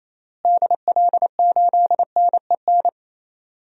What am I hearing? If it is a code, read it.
Morse code, heard as a keyed tone, about 28 wpm, 700 Hz. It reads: DL8NEN
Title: DXCC-100 Call Signs - 6 Characters - Repeated 3x - 28wpm